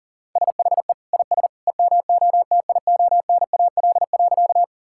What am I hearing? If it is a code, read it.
Morse code, heard as a keyed tone, about 40 wpm, 700 Hz. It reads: SHE IS EMOTIONAL.
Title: Sentences from Top 1000 Words - Review Part 5 - 40wpm